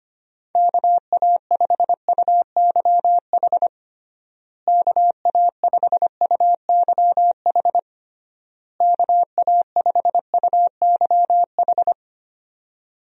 Morse code, heard as a keyed tone, about 25 wpm, 700 Hz. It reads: KA5UYH KA5UYH KA5UYH
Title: USA Call Signs - 6 Characters - Repeated 3x - 25wpm